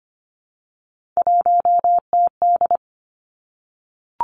Morse code, heard as a keyed tone, about 25 wpm, 700 Hz. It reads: 1TD
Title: Three Challenging Letter-Number Combinations 25wpm